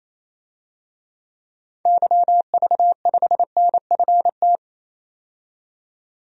Morse code, heard as a keyed tone, about 28 wpm, 700 Hz. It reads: YV5NFT
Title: DXCC-100 Call Signs - 6 Characters - 28wpm